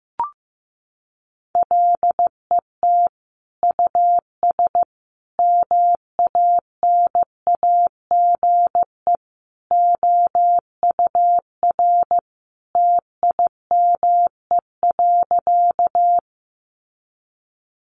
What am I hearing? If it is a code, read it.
Morse code, heard as a keyed tone, about 15 wpm, 700 Hz. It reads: LET US MANAGE OUR TIME.
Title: Sentences from Top 1400 Words 15wpm